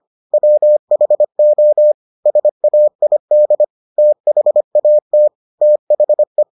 Morse code, heard as a keyed tone, about 25 wpm, 600 Hz. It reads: WHO SAID THAT THE